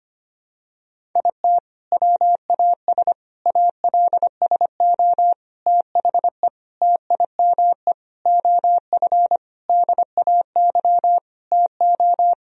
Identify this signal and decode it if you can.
Morse code, heard as a keyed tone, about 25 wpm, 700 Hz. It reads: IT WAS ALSO THE TIME OF DAY TO
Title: Sentences from Top 1000 Words - Review Part 1 - 25wpm